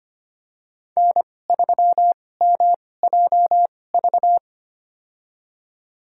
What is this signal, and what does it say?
Morse code, heard as a keyed tone, about 25 wpm, spaced as for 18 wpm, 700 Hz. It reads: N3MJV
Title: USA Call Signs - 5 Characters - 2x Character Spacing - 25wpm